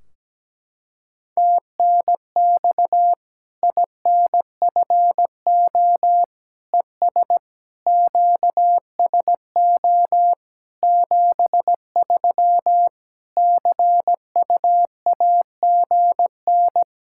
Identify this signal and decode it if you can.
Morse code, heard as a keyed tone, about 17 wpm, 700 Hz. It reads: TNX INFO ES QSO 73 CUAGN
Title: Practice QSO - Exchange 4 - Ending - 17wpm